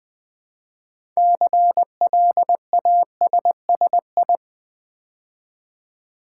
Morse code, heard as a keyed tone, about 20 wpm, 700 Hz. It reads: CLASSI
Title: Build A Word - 3 Letters At A Time - 8-Letter Words - 20wpm